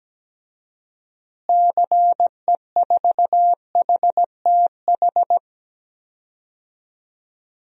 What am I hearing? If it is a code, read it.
Morse code, heard as a keyed tone, about 17 wpm, 700 Hz. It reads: CE4HTH
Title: DXCC-100 Call Signs - 6 Characters - Repeated 2x - 17wpm